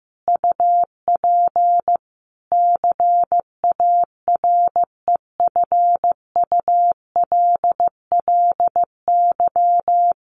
Morse code, heard as a keyed tone, about 15 wpm, 700 Hz. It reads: UP CAREFULLY
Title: Sentences from Top 1300 Words 15wpm